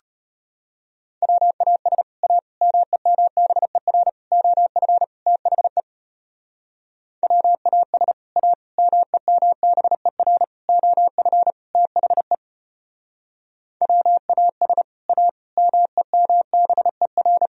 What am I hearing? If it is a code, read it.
Morse code, heard as a keyed tone, about 32 wpm, 700 Hz. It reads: WAS A MEMBER OF THE WAS A MEMBER OF THE WAS A MEMBER